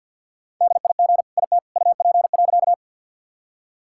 Morse code, heard as a keyed tone, about 50 wpm, 700 Hz. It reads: DIG IT UP.
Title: Sentences from Top 1600 Words 50wpm